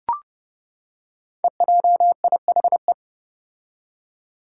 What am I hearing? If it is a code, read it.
Morse code, heard as a keyed tone, about 30 wpm, 700 Hz. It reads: EJIHE
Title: Five Challenging Letter-Number Combinations 30wpm